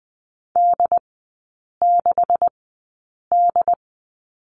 Morse code, heard as a keyed tone, about 20 wpm, spaced as for 8 wpm, 700 Hz. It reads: D6D
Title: 138 Lesson 23e - Letter f - Three Characters 20wpm